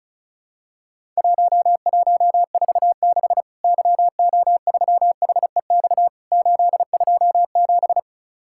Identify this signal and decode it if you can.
Morse code, heard as a keyed tone, about 36 wpm, 700 Hz. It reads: 1146 YO3HEX 827
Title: Worked All Europe DX Contest - QTC Exchange - 35wpm